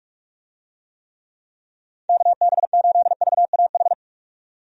Morse code, heard as a keyed tone, about 46 wpm, 700 Hz. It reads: KB8VAH
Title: USA Call Signs - 6 Characters - 45wpm